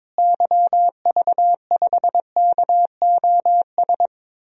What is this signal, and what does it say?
Morse code, heard as a keyed tone, about 22 wpm, 700 Hz. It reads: YV5KOS
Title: DXCC-100 Call Signs - 6 Characters - Repeated 2x - 22wpm